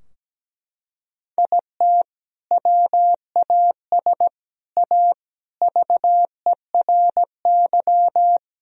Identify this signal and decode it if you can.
Morse code, heard as a keyed tone, about 17 wpm, 700 Hz. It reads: IT WAS A VERY
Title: Sentences from Top 1000 Words - Review Part 3 - 17wpm